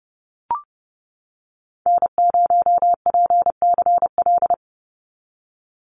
Morse code, heard as a keyed tone, about 31 wpm, 700 Hz. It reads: N0PCL
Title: NA SOTA Activator Callsigns 30wpm